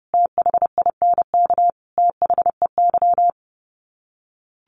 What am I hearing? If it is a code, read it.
Morse code, heard as a keyed tone, about 28 wpm, 700 Hz. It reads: THINK THEY
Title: Sets of 5 Words 30wpm